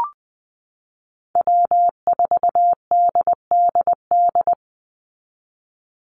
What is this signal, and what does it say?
Morse code, heard as a keyed tone, about 20 wpm, 700 Hz. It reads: W4DDD
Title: Call Signs Repeated 2x - 20wpm